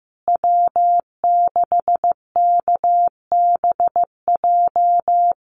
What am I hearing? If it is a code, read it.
Morse code, heard as a keyed tone, about 15 wpm, 700 Hz. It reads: W6KBJ
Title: USA Call Signs Repeated 3x 15wpm